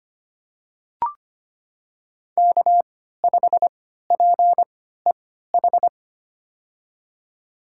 Morse code, heard as a keyed tone, about 25 wpm, spaced as for 14 wpm, 700 Hz. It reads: K5PEH
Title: USA Call Signs - 5 Characters - 3x Character Spacing - 25wpm